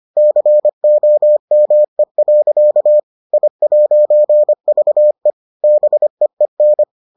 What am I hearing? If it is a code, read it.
Morse code, heard as a keyed tone, about 25 wpm, 600 Hz. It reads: COME. I'VE BEEN